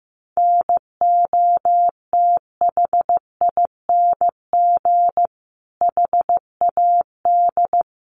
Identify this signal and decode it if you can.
Morse code, heard as a keyed tone, about 15 wpm, 700 Hz. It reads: NOTHING HAD
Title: Sentences from Top 1300 Words 15wpm